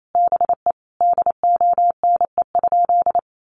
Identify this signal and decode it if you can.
Morse code, heard as a keyed tone, about 28 wpm, 700 Hz. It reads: BE DONE?
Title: Sentences from Top 500 Words 28wpm